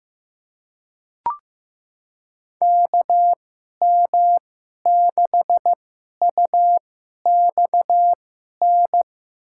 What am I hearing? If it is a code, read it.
Morse code, heard as a keyed tone, about 15 wpm, spaced as for 11 wpm, 700 Hz. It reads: KM6UXN
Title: USA Call Signs - 2x Character Spacing - 15wpm